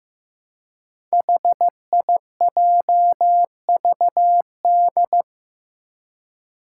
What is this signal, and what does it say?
Morse code, heard as a keyed tone, about 15 wpm, 700 Hz. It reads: HIJVD
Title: Five Challenging Letter-Number Combinations 15wpm